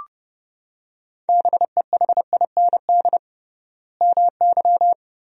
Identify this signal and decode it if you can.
Morse code, heard as a keyed tone, about 30 wpm, 700 Hz. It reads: BEHIND MY
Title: 3x Word-Spacing - Sentences from Top 500 Words - 30wpm